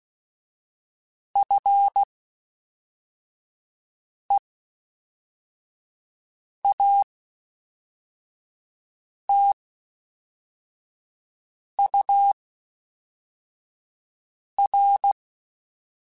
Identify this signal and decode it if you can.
Morse code, heard as a keyed tone, about 16 wpm, 800 Hz. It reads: F E A T U R
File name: Special INGs / Special INGs-mp3s / Fast ing @10